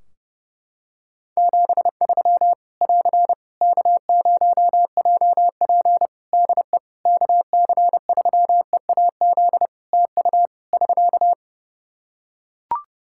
Morse code, heard as a keyed tone, about 30 wpm, 700 Hz. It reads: 73 <AR> K0JP DE KC3EAZ TU <SK>
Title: Practice QSO - Exchange 4 - Ending - 30wpm